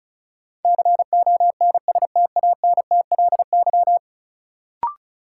Morse code, heard as a keyed tone, about 36 wpm, 700 Hz. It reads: CONSTANTLY
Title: Sentences from Top 1600 Words - New Word Drill Alt - 35wpm